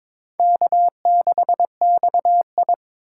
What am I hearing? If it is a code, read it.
Morse code, heard as a keyed tone, about 26 wpm, 700 Hz. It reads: K6XI
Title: Speed Racing - USA Call Signs 22wpm